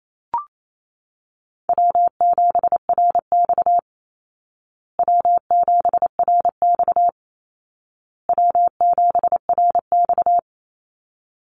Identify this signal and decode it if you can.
Morse code, heard as a keyed tone, about 28 wpm, 700 Hz. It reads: W7RX W7RX W7RX
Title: Call Signs Repeated 2x - 28wpm